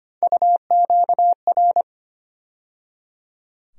Morse code, heard as a keyed tone, about 25 wpm, 700 Hz. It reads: UQR
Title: USA Call Sign Postfixes - 3 Letters 25wpm